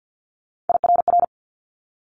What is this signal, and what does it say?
Morse code, heard as a keyed tone, about 50 wpm, 700 Hz. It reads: IRR
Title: Multi-letter Phonemes Quick Fire 50wpm